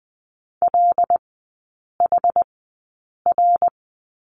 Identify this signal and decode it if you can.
Morse code, heard as a keyed tone, about 20 wpm, spaced as for 8 wpm, 700 Hz. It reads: LHR
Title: 074 Lesson 13e - Letter L - Three Characters 20wpm